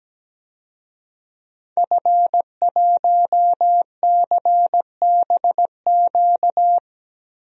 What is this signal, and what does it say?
Morse code, heard as a keyed tone, about 17 wpm, 700 Hz. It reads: F1CBQ
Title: DXCC-100 Call Signs - 5 Characters - 17wpm